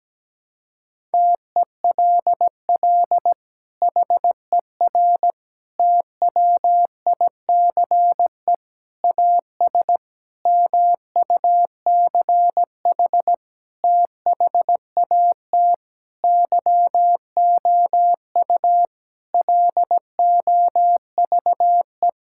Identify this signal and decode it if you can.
Morse code, heard as a keyed tone, about 17 wpm, 700 Hz. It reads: TELL HER TWICE AS MUCH THAT YOU LOVE
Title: Sentences from Top 1600 Words 17wpm